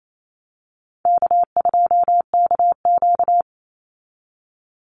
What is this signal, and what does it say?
Morse code, heard as a keyed tone, about 28 wpm, 700 Hz. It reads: K2KQ
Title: Call Signs - 28wpm